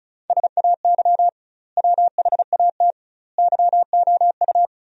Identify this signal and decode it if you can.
Morse code, heard as a keyed tone, about 36 wpm, 700 Hz. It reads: SAY WHAT YOU
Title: 2x Word-Spacing - Sentences from Top 200 Words - 35wpm